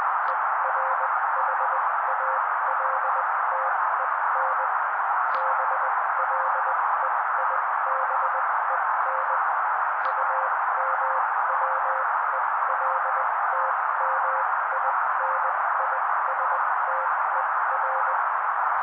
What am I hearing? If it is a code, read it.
Morse code, heard as a keyed tone, about 20 wpm, spaced as for 16 wpm, 530 Hz. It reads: ERHALTEN BLEIBEN UMWELTMINISTER
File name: daily / 16wpm noise